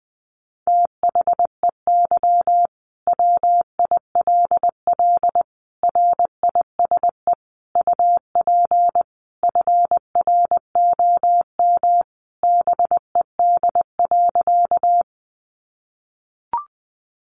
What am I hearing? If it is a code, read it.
Morse code, heard as a keyed tone, about 20 wpm, 700 Hz. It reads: THEY WILL RISE UP FROM BED.
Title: Sentences from Top 900 Words 20wpm